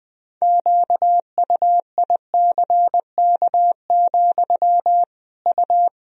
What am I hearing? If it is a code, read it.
Morse code, heard as a keyed tone, about 20 wpm, 700 Hz. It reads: QUICK, U
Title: Sentences from Top 500 Words and Abbreviations 20wpm